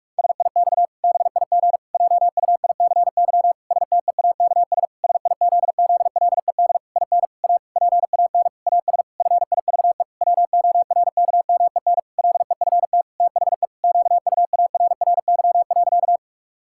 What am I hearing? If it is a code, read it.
Morse code, heard as a keyed tone, about 46 wpm, 700 Hz. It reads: SIX BIG JUICY STEAKS SIZZLED IN A PAN AS FIVE WORKMEN LEFT THE QUARRY.
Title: Pangram Challenge v2 - 45wpm